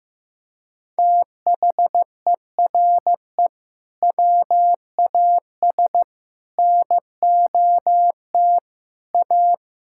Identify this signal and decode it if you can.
Morse code, heard as a keyed tone, about 15 wpm, 700 Hz. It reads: THERE WAS NOT A
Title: Sentences from Top 1000 Words - Review Part 6 - 15wpm